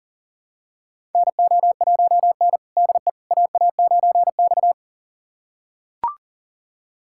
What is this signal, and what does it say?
Morse code, heard as a keyed tone, about 40 wpm, 700 Hz. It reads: NO1N DE AA9X
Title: USA Callsign DE Callsign - 4 Characters - 40wpm